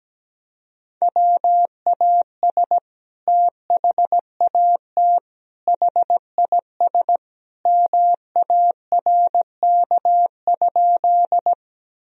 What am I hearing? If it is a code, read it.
Morse code, heard as a keyed tone, about 17 wpm, 700 Hz. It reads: WAS THAT HIS MARK?
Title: Sentences from Top 1000 Words - Review Part 1 - 17wpm